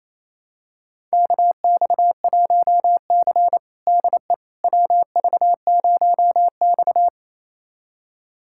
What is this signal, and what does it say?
Morse code, heard as a keyed tone, about 28 wpm, 700 Hz. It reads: KX1C DE WV0X
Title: USA Callsign DE Callsign - 4 Characters - 28wpm